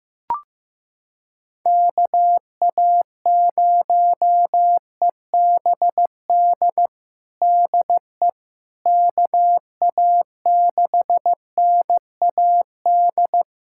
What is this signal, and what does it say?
Morse code, heard as a keyed tone, about 15 wpm, 700 Hz. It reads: KA0EBD DE KA6NAD
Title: USA Callsign DE Callsign - 6 Characters - 15wpm